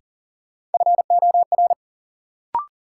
Morse code, heard as a keyed tone, about 40 wpm, 700 Hz. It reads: FOR
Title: Sets of 2 Words - New Word Drill - 40wpm